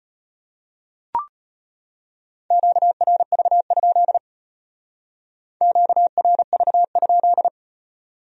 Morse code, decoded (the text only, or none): QRV? QRV?